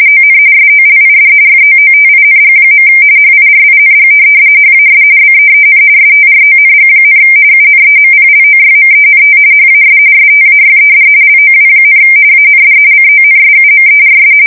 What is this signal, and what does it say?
RTTY, about 45.45 baud. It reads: ON SOLID FOOTING. HIRING HAS BEEN STRONG, UNEMPLOYMENT HAS STAYED LOW, AND CONSUMERS
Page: 170 Shift 2125/2295